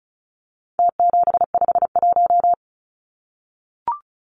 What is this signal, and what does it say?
Morse code, heard as a keyed tone, about 34 wpm, 700 Hz. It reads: T751
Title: Numbers - Contest Style - 4-Digit 35wpm